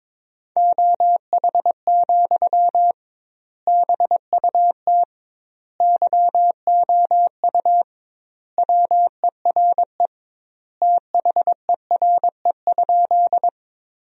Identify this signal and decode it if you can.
Morse code, heard as a keyed tone, about 22 wpm, 700 Hz. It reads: OH, BUT YOU WERE THERE?
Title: 2x Word-Spacing - Sentences from Top 500 Words - 22wpm